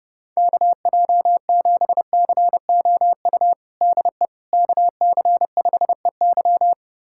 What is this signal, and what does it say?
Morse code, heard as a keyed tone, about 30 wpm, 700 Hz. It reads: KJ7COU DE KC5EY
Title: USA Callsign DE Callsign 30wpm